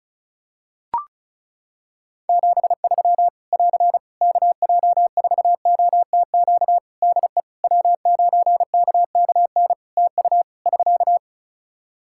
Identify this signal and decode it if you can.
Morse code, heard as a keyed tone, about 36 wpm, 700 Hz. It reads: 73 <AR> KJ4OTQ DE W9KKN TU <SK>
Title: Practice QSO - Exchange 4 - Ending - 35wpm